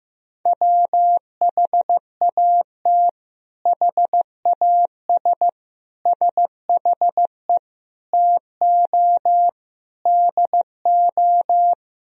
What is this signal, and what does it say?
Morse code, heard as a keyed tone, about 15 wpm, 700 Hz. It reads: WHAT HAS SHE TO DO
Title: Sentences from Top 100 Words and Abbreviations 15wpm